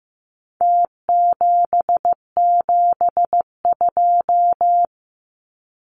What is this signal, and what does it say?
Morse code, heard as a keyed tone, about 15 wpm, 700 Hz. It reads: T772
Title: Numbers - Contest Style - 4-Digit 15wpm